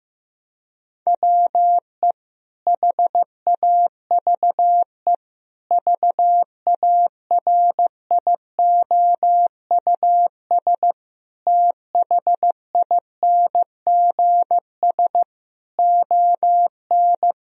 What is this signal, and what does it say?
Morse code, heard as a keyed tone, about 16 wpm, 700 Hz. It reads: WE HAVE VARIOUS THINGS ON
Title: Sentences from Top 1000 Words - Review Part 4 - 15wpm